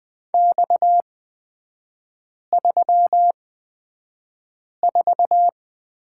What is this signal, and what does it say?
Morse code, heard as a keyed tone, about 20 wpm, spaced as for 5 wpm, 690 Hz. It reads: X34
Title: 229 Lesson 37e - Letter x - Three Characters 20wpm